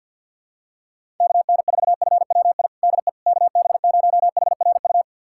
Morse code, heard as a keyed tone, about 52 wpm, 700 Hz. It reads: KN4FWI DE KB0HRU
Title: USA Callsign DE Callsign - 6 Characters - 50wpm